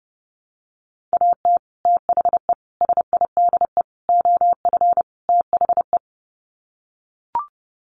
Morse code, heard as a keyed tone, about 30 wpm, 700 Hz. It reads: AT THE SIDE OF THE
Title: Sets of 5 Words Encore2 30wpm